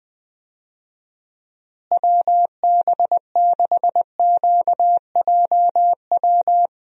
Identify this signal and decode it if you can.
Morse code, heard as a keyed tone, about 21 wpm, 700 Hz. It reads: WB6QJW
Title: USA Call Signs - 6 Characters - Repeated 3x - 20wpm